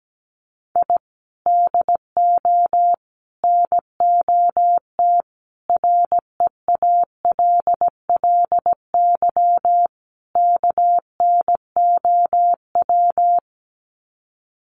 Morse code, heard as a keyed tone, about 17 wpm, 700 Hz. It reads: I DO NOT REALLY KNOW
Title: Sets of 5 Words Encore 17wpm